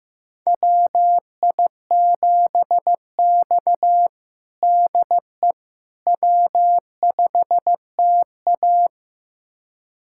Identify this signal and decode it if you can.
Morse code, heard as a keyed tone, about 16 wpm, 700 Hz. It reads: WI7X DE W5TA
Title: USA Callsign DE Callsign - 4 Characters - 15wpm